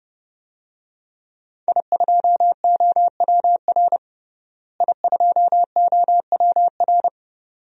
Morse code, heard as a keyed tone, about 32 wpm, 700 Hz. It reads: I2OWR I2OWR
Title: DXCC-100 Call Signs - 5 Characters - Repeated 2x - 30wpm